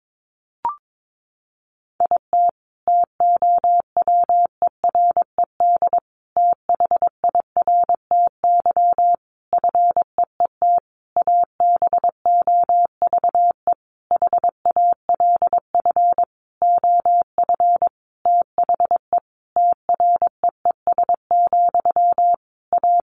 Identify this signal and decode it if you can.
Morse code, heard as a keyed tone, about 22 wpm, 700 Hz. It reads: IT TOWERED THIRTY FEET ABOVE HALF OF THE TREES, A
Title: A Sound Of Thunder 22wpm